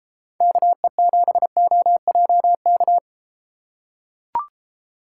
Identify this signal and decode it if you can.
Morse code, heard as a keyed tone, about 31 wpm, 700 Hz. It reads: KE7OJK